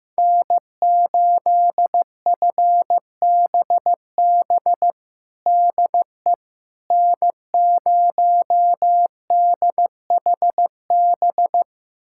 Morse code, heard as a keyed tone, about 15 wpm, 700 Hz. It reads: N8FBB DE N0DHB
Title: USA Callsign DE Callsign - 5 Characters - 15wpm